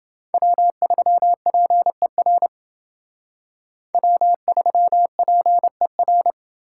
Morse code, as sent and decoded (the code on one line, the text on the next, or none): .-- ...-- .--. .-. / .-- ...-- .--. .-.
W3PER W3PER